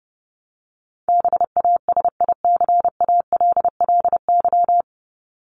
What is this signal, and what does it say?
Morse code, heard as a keyed tone, about 30 wpm, 700 Hz. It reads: BASICALLY
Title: Top 1251-1500 Words 30wpm